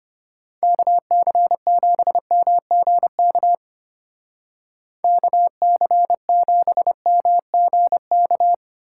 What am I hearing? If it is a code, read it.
Morse code, heard as a keyed tone, about 33 wpm, 700 Hz. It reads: KC7MGK KC7MGK